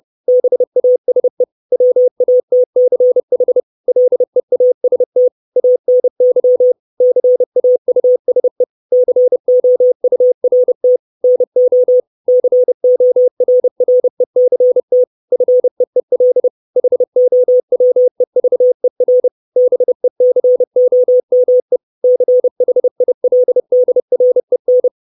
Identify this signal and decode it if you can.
Morse code, heard as a keyed tone, about 30 wpm, 500 Hz. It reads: BASE WATCH LEAST ANY CAUSE COURT NO CORRECT FEEL HOWEVER BECOME CHILDREN
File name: 632 Words 30WPM